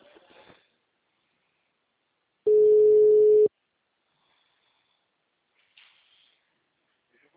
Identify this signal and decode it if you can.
Ringback tone (Europe, 425 Hz).
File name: out- -2012- - - .56416